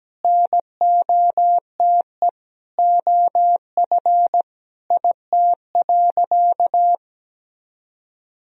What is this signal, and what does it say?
Morse code, heard as a keyed tone, about 17 wpm, 700 Hz. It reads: NOTE OF IT.
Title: Sentences from Top 500 Words and Abbreviations 17wpm